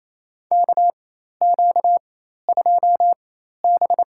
Morse code, heard as a keyed tone, about 28 wpm, spaced as for 13 wpm, 700 Hz. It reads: KQ2B
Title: USA Call Signs - 4 Characters - 4x Character Spacing - 28wpm